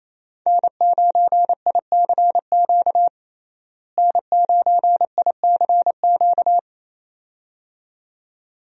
Morse code, heard as a keyed tone, about 28 wpm, 700 Hz. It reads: N9ICQ N9ICQ
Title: USA Call Signs - 5 Characters - Repeated 2x - 28wpm